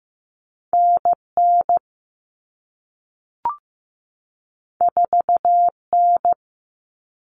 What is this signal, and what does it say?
Morse code, heard as a keyed tone, about 15 wpm, 700 Hz. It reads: NN 4N
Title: All Cut Numbers - 2-Digit 15wpm